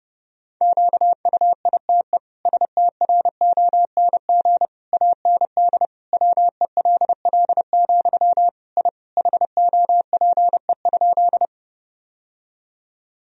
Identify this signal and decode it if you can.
Morse code, heard as a keyed tone, about 32 wpm, 700 Hz. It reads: QUITE STRONG AND WELL, I HOPE?
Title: Sentences from Top 1000 Words - Review Part 3 - 30wpm